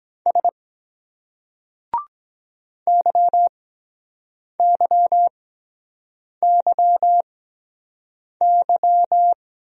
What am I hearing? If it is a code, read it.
Morse code, heard as a keyed tone, about 23 wpm, spaced as for 7 wpm, 700 Hz. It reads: S YYYY